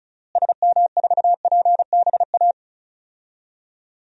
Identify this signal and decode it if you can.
Morse code, heard as a keyed tone, about 36 wpm, 700 Hz. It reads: SM4PBA
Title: DXCC-100 Call Signs - 6 Characters - Repeated 3x - 35wpm